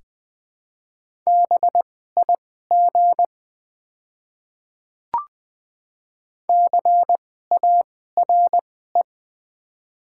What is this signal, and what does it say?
Morse code, heard as a keyed tone, about 20 wpm, spaced as for 14 wpm, 690 Hz. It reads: BIG CARE
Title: 2x Character Spacing - Top 500 Words - 20wpm